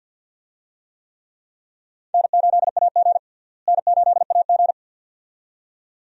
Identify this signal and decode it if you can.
Morse code, heard as a keyed tone, about 49 wpm, 700 Hz. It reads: N8AG N8AG
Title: USA Call Signs - 4 Characters - Repeated 2x - 50wpm